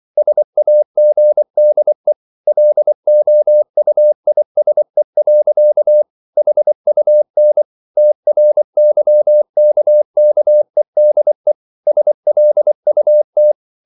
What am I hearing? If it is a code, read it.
Morse code, heard as a keyed tone, about 24 wpm, 600 Hz. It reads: SAGDE LOUISE. HUN TRYKKEDE SLUT
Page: Ferie 24 wpm